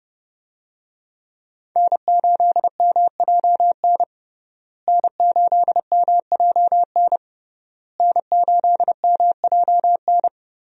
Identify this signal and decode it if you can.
Morse code, heard as a keyed tone, about 30 wpm, 700 Hz. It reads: N8MJN N8MJN N8MJN
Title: USA Call Signs - 5 Characters - Repeated 3x - 30wpm